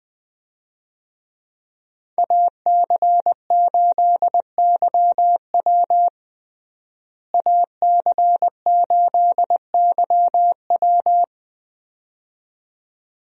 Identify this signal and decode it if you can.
Morse code, heard as a keyed tone, about 20 wpm, 700 Hz. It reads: AC8YW AC8YW
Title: USA Call Signs - 5 Characters - Repeated 2x - 20wpm